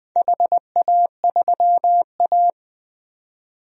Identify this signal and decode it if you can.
Morse code, heard as a keyed tone, about 20 wpm, 700 Hz. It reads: HA3A
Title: DXCC-100 Call Signs - 4 Characters - 20wpm